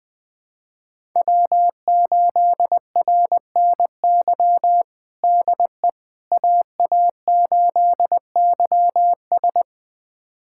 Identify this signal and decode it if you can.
Morse code, heard as a keyed tone, about 20 wpm, 700 Hz. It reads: W8RNY DE AA8YS
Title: USA Callsign DE Callsign 20wpm